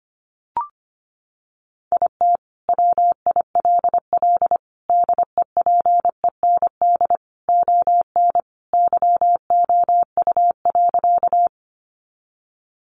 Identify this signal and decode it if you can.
Morse code, heard as a keyed tone, about 25 wpm, 700 Hz. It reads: IT WILL DEPEND ON YOU.
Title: Sentences from Top 1300 Words 25wpm